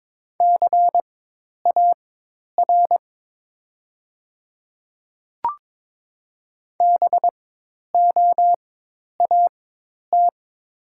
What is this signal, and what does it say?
Morse code, heard as a keyed tone, about 23 wpm, spaced as for 10 wpm, 700 Hz. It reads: CAR BOAT
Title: 4x Character Spacing - Top 500 Words - 22wpm